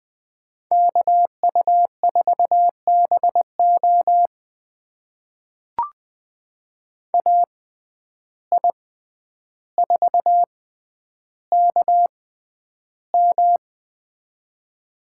Morse code, heard as a keyed tone, about 20 wpm, spaced as for 10 wpm, 700 Hz. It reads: KU4BO AI4KM
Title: USA Call Signs - 5 Characters - 6x Character Spacing - 20wpm